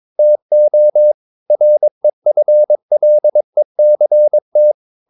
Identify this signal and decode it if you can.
Morse code, heard as a keyed tone, about 22 wpm, 600 Hz. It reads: TO REFLECT
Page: CW med Gnister